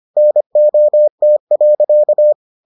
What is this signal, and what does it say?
Morse code, heard as a keyed tone, about 25 wpm, 600 Hz. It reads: NOT.